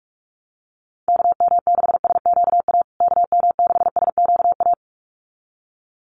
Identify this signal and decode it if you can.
Morse code, heard as a keyed tone, about 44 wpm, 700 Hz. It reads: KM6SQA KM6SQA
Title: USA Call Signs Repeated 2x 45wpm